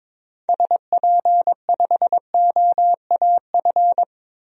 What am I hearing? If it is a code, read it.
Morse code, heard as a keyed tone, about 22 wpm, 700 Hz. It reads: SP5OAF
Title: DXCC-100 Call Signs - 6 Characters - 22wpm